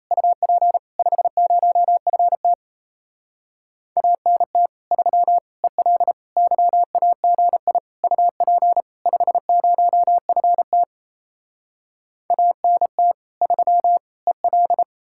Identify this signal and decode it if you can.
Morse code, heard as a keyed tone, about 38 wpm, 700 Hz. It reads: UP 50FT ANT 3 EL YAGI UP 50FT ANT 3 EL